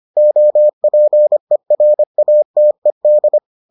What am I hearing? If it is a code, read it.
Morse code, heard as a keyed tone, about 25 wpm, 600 Hz. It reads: OPERATED